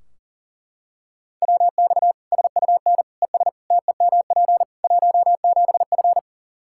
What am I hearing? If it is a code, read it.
Morse code, heard as a keyed tone, about 37 wpm, 700 Hz. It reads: WX SUN ES TEMP 17F
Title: K1USN SST QSO Practice 40wpm